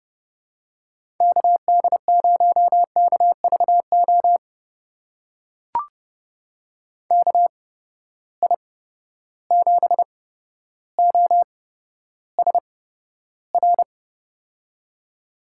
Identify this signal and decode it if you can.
Morse code, heard as a keyed tone, about 30 wpm, spaced as for 13 wpm, 700 Hz. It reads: KD0KVO KI7OSR
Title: USA Call Signs - 6 Characters - 8x Character Spacing - 30wpm